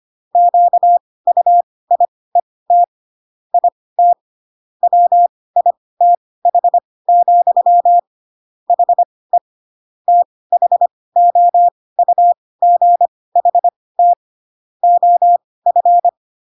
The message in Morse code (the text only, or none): QUIET IT WITH, HE THOUGHT OF